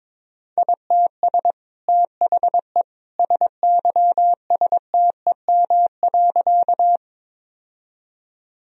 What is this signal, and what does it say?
Morse code, heard as a keyed tone, about 22 wpm, 700 Hz. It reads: ITS THE SYSTEM.
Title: Sentences from Top 1000 Words - Review Part 3 - 22wpm